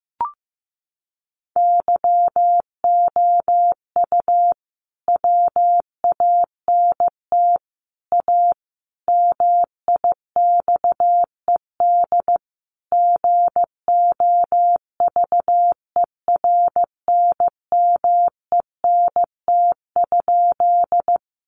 Morse code, heard as a keyed tone, about 15 wpm, 700 Hz. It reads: YOU WANT A MIXED GOVERNMENT?
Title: Sentences from Top 1500 Words 15wpm